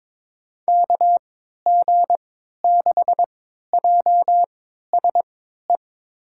Morse code, heard as a keyed tone, about 22 wpm, spaced as for 13 wpm, 700 Hz. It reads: KG6JSE
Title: USA Call Signs - 6 Characters - 3x Character Spacing - 22wpm